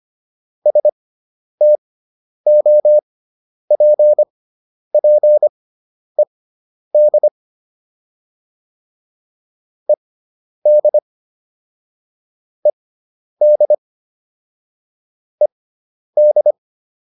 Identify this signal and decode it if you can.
Morse code, heard as a keyed tone, about 25 wpm, spaced as for 10 wpm, 600 Hz. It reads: STOPPED ED ED ED